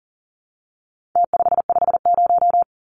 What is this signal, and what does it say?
Morse code, heard as a keyed tone, about 40 wpm, 700 Hz. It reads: T550
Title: Numbers - Contest Style - 4-Digit 40wpm